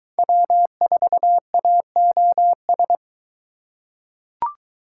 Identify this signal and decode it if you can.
Morse code, heard as a keyed tone, about 23 wpm, 700 Hz. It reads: W4AOS